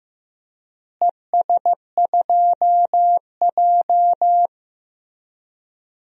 Morse code, heard as a keyed tone, about 15 wpm, 700 Hz. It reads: ES2J
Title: DXCC-100 Call Signs - 4 Characters - Repeated 2x - 15wpm